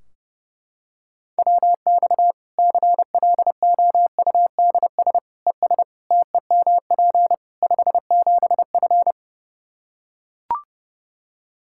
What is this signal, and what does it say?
Morse code, heard as a keyed tone, about 30 wpm, 700 Hz. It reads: WX CLOUDS ES TEMP 57F
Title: Practice QSO - Exchange 2 - Info - 30wpm